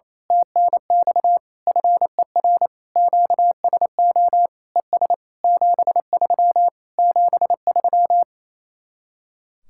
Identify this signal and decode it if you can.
Morse code, heard as a keyed tone, about 27 wpm, 700 Hz. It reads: TNX FER QSO ES 73 73
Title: Speed-Racing Common QSO Phrases - 28wpm